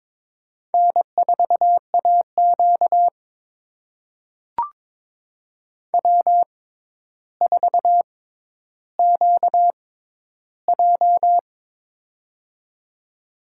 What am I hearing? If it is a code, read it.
Morse code, heard as a keyed tone, about 22 wpm, spaced as for 11 wpm, 700 Hz. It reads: N4AQ W4QJ
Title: USA Call Signs - 4 Characters - 6x Character Spacing - 22wpm